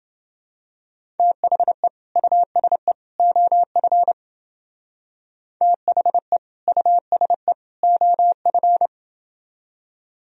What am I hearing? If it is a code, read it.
Morse code, heard as a keyed tone, about 30 wpm, 700 Hz. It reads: THE USE OF THE USE OF